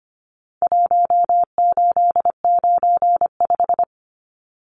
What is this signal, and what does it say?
Morse code, heard as a keyed tone, about 25 wpm, 700 Hz. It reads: 1895
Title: Numbers - Contest Style - 4-Digit 25wpm